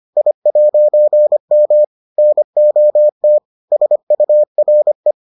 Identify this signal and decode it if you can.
Morse code, heard as a keyed tone, about 25 wpm, 600 Hz. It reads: I'M NOT SURE